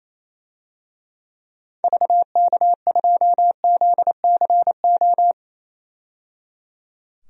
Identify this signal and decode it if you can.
Morse code, heard as a keyed tone, about 28 wpm, 700 Hz. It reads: VK2ZCO
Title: DXCC-100 Call Signs - 6 Characters - 28wpm